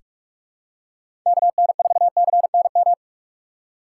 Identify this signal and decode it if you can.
Morse code, heard as a keyed tone, about 44 wpm, 700 Hz. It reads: KN4CNM
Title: USA Call Signs - 6 Characters - 2x Character Spacing - 45wpm